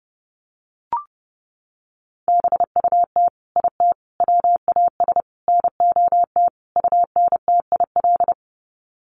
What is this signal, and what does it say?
Morse code, heard as a keyed tone, about 29 wpm, 700 Hz. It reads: BUT IT WAS NOT UNTIL
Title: Sets of 5 Words Encore2 30wpm